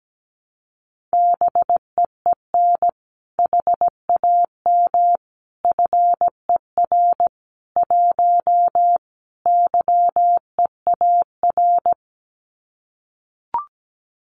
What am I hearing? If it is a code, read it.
Morse code, heard as a keyed tone, about 17 wpm, 700 Hz. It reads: BEEN HAM FER 1 YEAR
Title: Common QSO Phrases Encore 17wpm